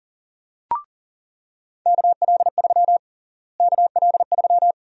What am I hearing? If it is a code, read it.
Morse code, heard as a keyed tone, about 40 wpm, 700 Hz. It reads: KL3 KL3
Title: USA Call Sign Prefixes - Repeated 2x - 40wpm